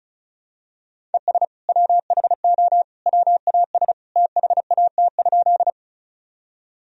Morse code, heard as a keyed tone, about 34 wpm, 700 Hz. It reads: ES WHO WAS THAT?
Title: Sentences from Top 100 Words and Abbreviations 35wpm